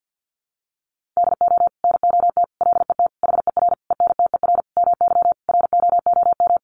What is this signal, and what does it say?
Morse code, heard as a keyed tone, about 50 wpm, 700 Hz. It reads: DO NOT LET HER ENTER MY ROOM
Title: Sentences from Top 1300 Words 50wpm